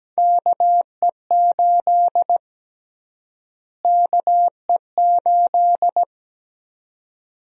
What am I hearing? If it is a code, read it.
Morse code, heard as a keyed tone, about 17 wpm, 700 Hz. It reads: KE8 KE8
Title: USA Call Sign Prefixes - Repeated 2x - 17wpm